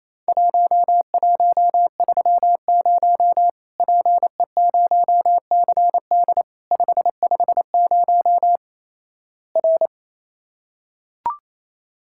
Morse code, heard as a keyed tone, about 27 wpm, 700 Hz. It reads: 1130 PE0CD 550 R
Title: Worked All Europe DX Contest - QTC Exchange - 28wpm